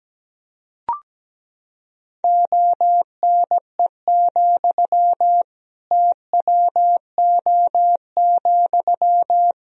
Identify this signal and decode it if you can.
Morse code, heard as a keyed tone, about 17 wpm, 700 Hz. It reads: ONE, TWO,
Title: Sentences from Top 1000 Words - Review Part 1 - 17wpm